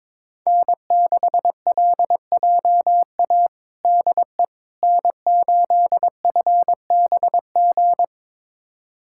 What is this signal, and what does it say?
Morse code, heard as a keyed tone, about 22 wpm, 700 Hz. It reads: N6LJA DE N8FBG
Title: USA Callsign DE Callsign - 5 Characters - 22wpm